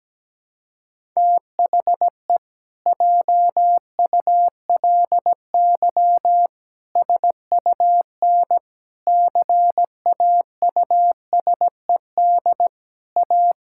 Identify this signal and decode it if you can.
Morse code, heard as a keyed tone, about 17 wpm, 700 Hz. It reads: THE JULY SUN CAUSED A
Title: Pangram Challenge v2 - 17wpm